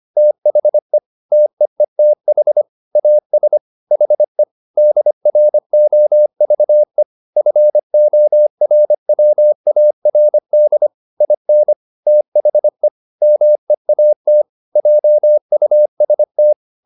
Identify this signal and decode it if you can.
Morse code, heard as a keyed tone, about 25 wpm, 600 Hz. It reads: THE TEETH AS HE DROVE FORWARD IN THE MEAT JUST